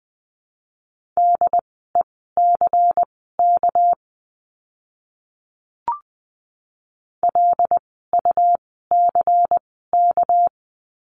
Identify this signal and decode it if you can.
Morse code, heard as a keyed tone, about 19 wpm, spaced as for 14 wpm, 700 Hz. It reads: DECK LUCK
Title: 199 Lesson 32f - Letter k - Words 20wpm